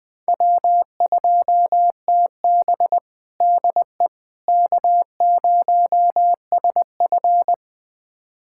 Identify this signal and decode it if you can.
Morse code, heard as a keyed tone, about 20 wpm, 700 Hz. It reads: W2TB DE K0SF
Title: USA Callsign DE Callsign - 4 Characters - 20wpm